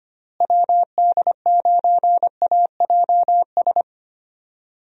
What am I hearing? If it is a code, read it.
Morse code, heard as a keyed tone, about 25 wpm, 700 Hz. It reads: WD9AJS
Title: USA Call Signs - 6 Characters - Repeated 3x - 25wpm